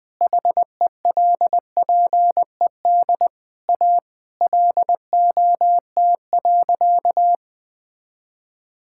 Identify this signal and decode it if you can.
Morse code, heard as a keyed tone, about 21 wpm, 700 Hz. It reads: HELPED A LOT.
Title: Sentences with 3-Word Phrases - 20wpm